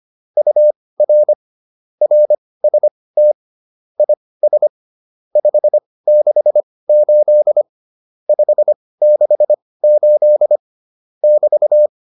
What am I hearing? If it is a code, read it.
Morse code, heard as a keyed tone, about 25 wpm, spaced as for 18 wpm, 600 Hz. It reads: UR RST IS 568 568 <BT>